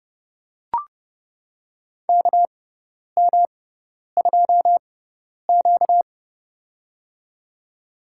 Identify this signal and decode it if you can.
Morse code, heard as a keyed tone, about 30 wpm, spaced as for 10 wpm, 700 Hz. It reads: KM2Q
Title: USA Call Signs - 6x Character Spacing - 30wpm